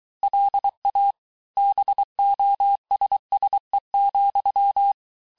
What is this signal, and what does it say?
Morse code, heard as a keyed tone, about 23 wpm, 800 Hz. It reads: LA BOSSE,
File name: cours-du-mercredi / audios / COURS-855--texte-clair-vitesse-22-mpm